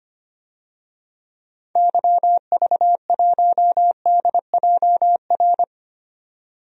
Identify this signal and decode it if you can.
Morse code, heard as a keyed tone, about 25 wpm, 700 Hz. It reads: YV1DJR
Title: DXCC-100 Call Signs - 6 Characters - 25wpm